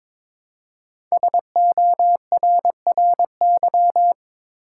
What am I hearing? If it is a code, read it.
Morse code, heard as a keyed tone, about 22 wpm, 700 Hz. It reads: SORRY
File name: 5 Letter Words - First 3 Letters Given - 22wpm